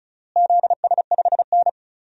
Morse code, heard as a keyed tone, about 36 wpm, 700 Hz. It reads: ZS5N
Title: DXCC-100 Call Signs - 4 Characters - Repeated 3x - 35wpm